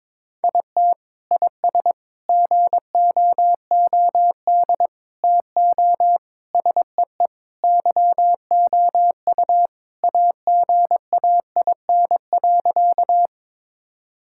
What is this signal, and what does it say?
Morse code, heard as a keyed tone, about 22 wpm, 700 Hz. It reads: IT IS GOOD TO SEE YOU AGAIN.
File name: practice / 22wpm / Sentences with 4-Word Phrases - 22wpm